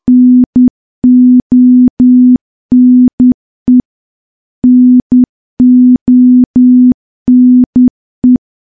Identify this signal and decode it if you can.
Morse code, heard as a keyed tone, about 10 wpm, 250 Hz. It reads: NONE NONE
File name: morse / morse-training-10WPM